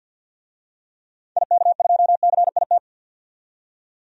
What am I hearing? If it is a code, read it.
Morse code, heard as a keyed tone, about 48 wpm, 700 Hz. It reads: IK2CIT
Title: DXCC-100 Call Signs - 6 Characters - 50wpm